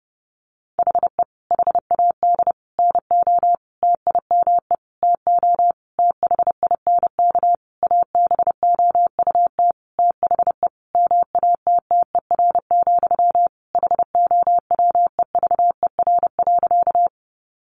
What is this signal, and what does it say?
Morse code, heard as a keyed tone, about 30 wpm, 700 Hz. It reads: HE HAD NO TIME TO THINK ABOUT THE MATTER, HOWEVER.
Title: Sentences from Top 700 Words 30wpm